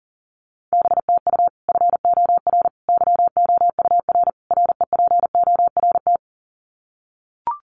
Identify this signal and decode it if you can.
Morse code, heard as a keyed tone, about 40 wpm, 700 Hz. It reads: BTU FOR YOUR REPORT
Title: Common QSO Phrases 40wpm